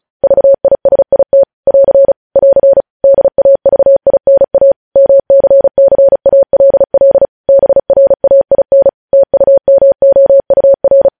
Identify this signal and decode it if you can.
Morse code, heard as a keyed tone, about 36 wpm, 550 Hz. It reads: VISIT <AR> <AR> DAVINA MCCALL BRAIN TUMOUR
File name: podcast / News headlines-1x-35x35